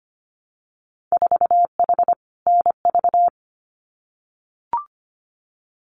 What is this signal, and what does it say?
Morse code, heard as a keyed tone, about 25 wpm, 690 Hz. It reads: 4H NV
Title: Winter Field Day Exchanges Encore 25wpm